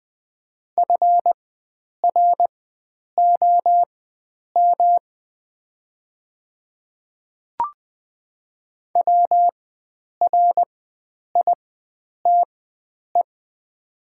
Morse code, heard as a keyed tone, about 20 wpm, spaced as for 9 wpm, 700 Hz. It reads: FROM WRITE
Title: 4x Character Spacing - Top 100 Words - 20wpm